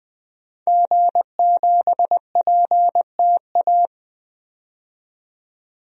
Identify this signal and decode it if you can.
Morse code, heard as a keyed tone, about 21 wpm, 700 Hz. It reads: G7PTA
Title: DXCC-100 Call Signs - 5 Characters - 20wpm